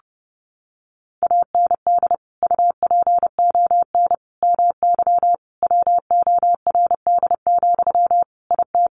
Morse code, heard as a keyed tone, about 30 wpm, 700 Hz. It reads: AND UPON MY WORD, IT
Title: Sentences from Top 1100 Words 30wpm